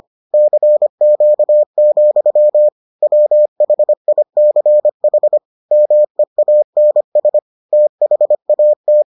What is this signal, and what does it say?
Morse code, heard as a keyed tone, about 25 wpm, 600 Hz. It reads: CQ, WHICH MEANS THAT